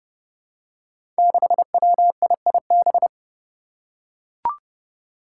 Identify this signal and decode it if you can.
Morse code, heard as a keyed tone, about 29 wpm, 700 Hz. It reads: 6WIIB
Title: Five Challenging Letter-Number Combinations 30wpm